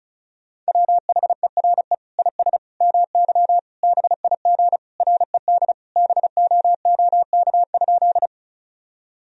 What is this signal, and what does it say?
Morse code, heard as a keyed tone, about 36 wpm, 700 Hz. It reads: WHERE IS MY BIG RED BOOK?
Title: Sentences from Top 1000 Words - Review Part 2 - 35wpm